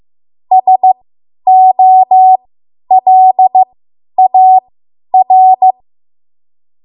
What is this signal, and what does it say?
Morse code, heard as a keyed tone, about 15 wpm, spaced as for 10 wpm, 750 Hz. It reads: SOLAR
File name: Morse / 10WPM